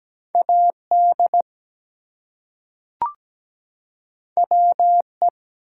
Morse code, heard as a keyed tone, about 17 wpm, 700 Hz. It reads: AD WE
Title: Two Letter Words - 17wpm